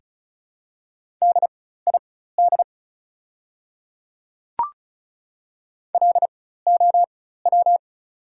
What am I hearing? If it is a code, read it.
Morse code, heard as a keyed tone, about 34 wpm, spaced as for 16 wpm, 690 Hz. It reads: DID LOW
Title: 4x Character Spacing - Top 200 Words - 35wpm